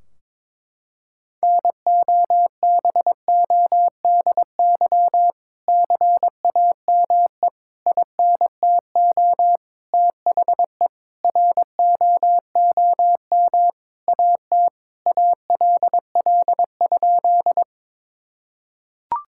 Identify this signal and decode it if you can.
Morse code, heard as a keyed tone, about 22 wpm, 700 Hz. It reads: NOBODY CAME INTO THE ROOM AT ALL?
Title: Sentences from Top 1600 Words 22wpm